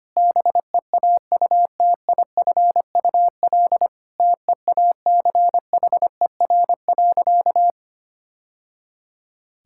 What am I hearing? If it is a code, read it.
Morse code, heard as a keyed tone, about 25 wpm, 700 Hz. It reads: BEAUTIFUL TEACHER.
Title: Sentences from Top 1000 Words - Review Part 6 - 25wpm